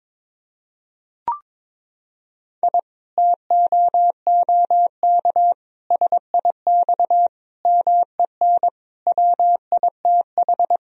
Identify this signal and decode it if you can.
Morse code, heard as a keyed tone, about 22 wpm, 700 Hz. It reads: I TOOK SIX MEN WITH
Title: Sentences from Top 1000 Words - Review Part 2 - 22wpm